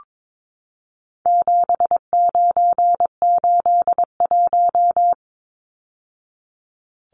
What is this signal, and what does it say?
Morse code, heard as a keyed tone, about 22 wpm, 700 Hz. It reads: 7981
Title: Numbers - 4-Digit 22wpm